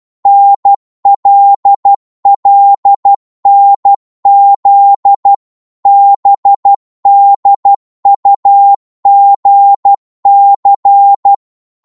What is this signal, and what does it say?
Morse code, heard as a keyed tone, about 12 wpm, 800 Hz. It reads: NLLNZ BDUGC